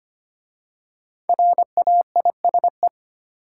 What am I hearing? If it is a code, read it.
Morse code, heard as a keyed tone, about 25 wpm, 700 Hz. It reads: RAISE
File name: 5 Letter Words - First 3 Letters Given - 25wpm